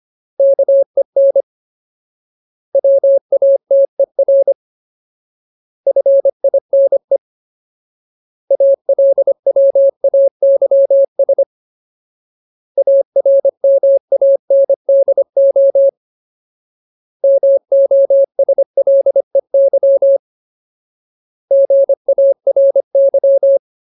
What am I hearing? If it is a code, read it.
Morse code, heard as a keyed tone, about 25 wpm, 550 Hz. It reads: KEN WATER FINE ALWAYS ARMANDO MOSLEY GARY
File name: Ultimate Word List / 25WPM Ultimate